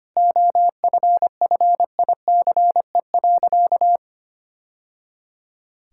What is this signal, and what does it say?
Morse code, heard as a keyed tone, about 25 wpm, 700 Hz. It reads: OFFICE.
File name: practice / 25wpm / Sentences with 2-Word Phrases - 25wpm